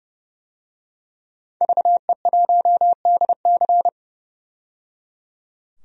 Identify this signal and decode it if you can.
Morse code, heard as a keyed tone, about 30 wpm, 700 Hz. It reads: VE1DC
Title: DXCC-100 Call Signs - 5 Characters - 30wpm